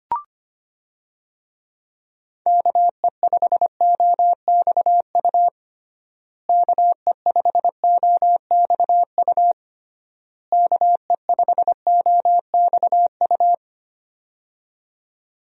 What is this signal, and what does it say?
Morse code, heard as a keyed tone, about 25 wpm, 700 Hz. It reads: KE5OXU KE5OXU KE5OXU
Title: USA Call Signs - 6 Characters - Repeated 3x - 25wpm